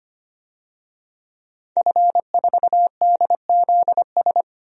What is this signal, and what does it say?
Morse code, heard as a keyed tone, about 25 wpm, 700 Hz. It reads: F4DZS
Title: DXCC-100 Call Signs - 5 Characters - Repeated 3x - 25wpm